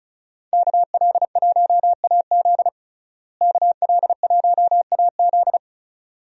Morse code, heard as a keyed tone, about 36 wpm, 700 Hz. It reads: KL1AZ KL1AZ
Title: USA Call Signs - 5 Characters - Repeated 3x - 35wpm